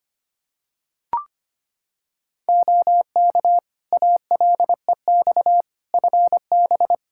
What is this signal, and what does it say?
Morse code, heard as a keyed tone, about 25 wpm, 700 Hz. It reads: OK ALEX FB
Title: Practice QSO - Exchange 2 - Info - 25wpm